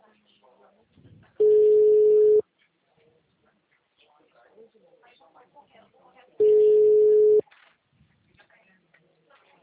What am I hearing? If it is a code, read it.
Ringback tone (Europe, 425 Hz), with speech.